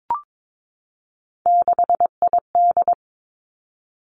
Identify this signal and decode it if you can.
Morse code, heard as a keyed tone, about 22 wpm, 700 Hz. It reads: 6ID
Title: Three Challenging Letter-Number Combinations 22wpm